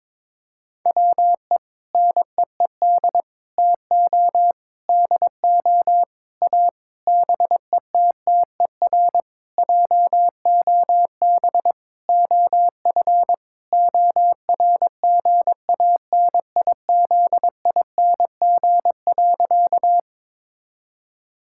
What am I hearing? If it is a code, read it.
Morse code, heard as a keyed tone, about 22 wpm, 700 Hz. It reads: WE NEED TO DO A BETTER JOB OF ORGANIZING.
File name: Sentences with 7-Word Phrases - 22wpm